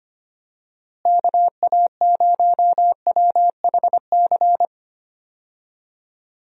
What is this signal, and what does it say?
Morse code, heard as a keyed tone, about 25 wpm, 700 Hz. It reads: KA0WHC
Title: USA Call Signs - 6 Characters - Repeated 2x - 25wpm